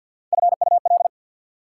Morse code, heard as a keyed tone, about 54 wpm, 690 Hz. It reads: FUL
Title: Prefixes and Suffixes 50wpm